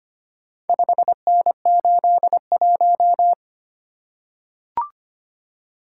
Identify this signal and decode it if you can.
Morse code, heard as a keyed tone, about 25 wpm, 700 Hz. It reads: 5N81
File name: practice / 25wpm / Four Letter-Number Combinations - 25wpm